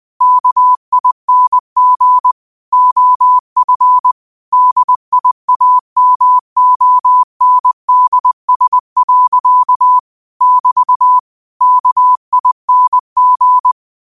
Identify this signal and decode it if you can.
Morse code, heard as a keyed tone, about 20 wpm, 1000 Hz. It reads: KING OF DIAMONDS. <BT> KING
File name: qotdfiles / 20wpm